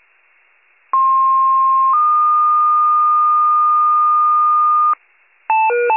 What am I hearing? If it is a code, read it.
Two-tone page, heard as a one-second tone, then a three-second tone, about 1060 then 1220 Hz, with noise. It is heard from 0.9 to 4.9 s.